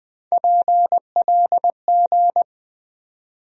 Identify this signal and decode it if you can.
Morse code, heard as a keyed tone, about 20 wpm, 700 Hz. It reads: PLG
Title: USA Call Sign Postfixes - 3 Letters 20wpm